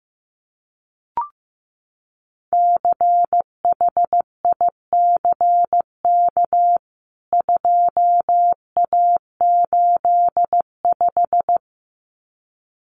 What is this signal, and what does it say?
Morse code, heard as a keyed tone, about 15 wpm, 700 Hz. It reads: CHICK 2A85
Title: CWT Exchange 15wpm